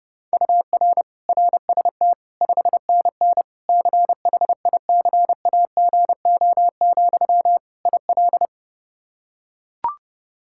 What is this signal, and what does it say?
Morse code, heard as a keyed tone, about 30 wpm, 700 Hz. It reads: UR RST 5NN CHICAGO, IL
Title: K1USN SST QSO Practice 30wpm